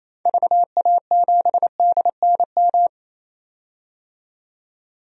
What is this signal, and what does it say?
Morse code, heard as a keyed tone, about 28 wpm, 700 Hz. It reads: VA7DNM
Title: DXCC-100 Call Signs - 6 Characters - 28wpm